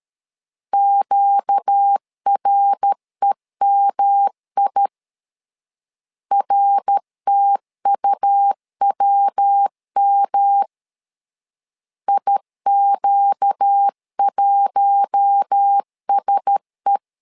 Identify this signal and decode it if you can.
Morse code, heard as a keyed tone, about 13 wpm, 800 Hz. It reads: QREMI RTUWM IQ1SE
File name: LEZIONI / 4 Q1S 12wpm 1802